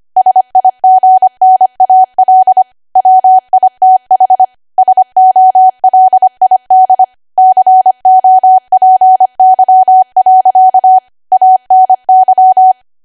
Morse code, heard as a keyed tone, about 25 wpm, 750 Hz. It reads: SIGNAL WITH SOLID COPY. ANY
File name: Archive / 25 WPM / 25WPM